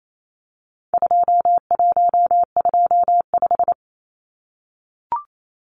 Morse code, heard as a keyed tone, about 28 wpm, 700 Hz. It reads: 2125
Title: Numbers - 4-Digit 28wpm